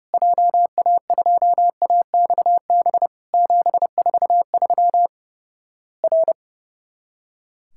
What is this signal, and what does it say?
Morse code, heard as a keyed tone, about 30 wpm, 700 Hz. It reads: JA2AXB 743 R
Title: Worked All Europe DX Contest - QTC Exchange - 30wpm